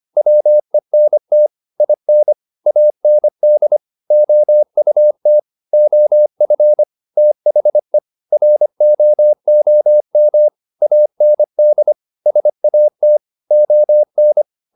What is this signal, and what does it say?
Morse code, heard as a keyed tone, about 25 wpm, 600 Hz. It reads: WENT IN AND OUT OF THE ROOM AND SAT ON